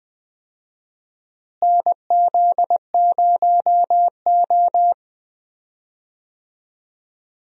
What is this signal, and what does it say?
Morse code, heard as a keyed tone, about 20 wpm, 700 Hz. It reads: NZ0O
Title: USA Call Signs - 4 Characters - Repeated 2x - 20wpm